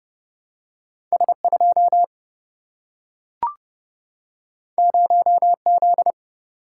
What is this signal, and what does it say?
Morse code, heard as a keyed tone, about 30 wpm, 700 Hz. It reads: S2 0Z
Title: Single-Letter Single-Number 30wpm